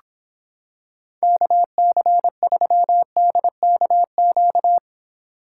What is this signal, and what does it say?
Morse code, heard as a keyed tone, about 26 wpm, 700 Hz. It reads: KC3DKQ